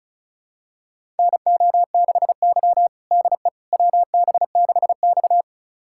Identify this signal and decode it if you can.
Morse code, heard as a keyed tone, about 36 wpm, 700 Hz. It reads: NO6Y DE WB6X
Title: USA Callsign DE Callsign - 4 Characters - 35wpm